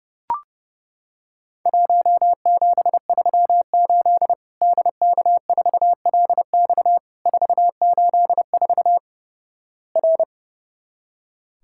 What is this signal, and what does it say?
Morse code, heard as a keyed tone, about 30 wpm, 700 Hz. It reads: 1738 DK4LX 484 R
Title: Worked All Europe DX Contest - QTC Exchange - 30wpm